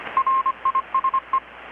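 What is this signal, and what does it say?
Morse code, heard as a keyed tone, about 25 wpm, 1060 Hz, with radio noise. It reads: RISE
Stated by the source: Morse from KPH's Night of Nights XX maritime radiotelegraphy event, received in Chiapas, Mexico about 3,800 km away on a Kenwood R-600 with a grounded 10 m long-wire antenna.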